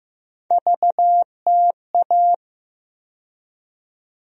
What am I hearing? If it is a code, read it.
Morse code, heard as a keyed tone, about 15 wpm, 700 Hz. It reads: VTA
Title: USA Call Sign Postfixes - 3 Letters 15wpm